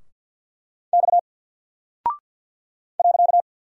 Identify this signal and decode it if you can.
Morse code, heard as a keyed tone, about 48 wpm, 700 Hz. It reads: X .
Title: Single Character (Letter:Number:Punctuation:Prosign) Alt 50wpm